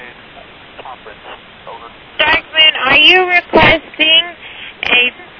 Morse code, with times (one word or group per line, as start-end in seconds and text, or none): none